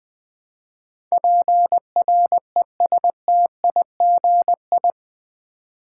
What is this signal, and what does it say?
Morse code, heard as a keyed tone, about 20 wpm, 700 Hz. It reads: PRESTIGI
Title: Build A Word - 4 Letters At A Time - 11-Letter Words - 20wpm